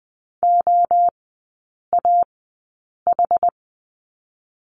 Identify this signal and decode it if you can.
Morse code, heard as a keyed tone, about 20 wpm, spaced as for 8 wpm, 690 Hz. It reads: OAH
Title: 060 Lesson 11e - Letter h - Three Characters 20wpm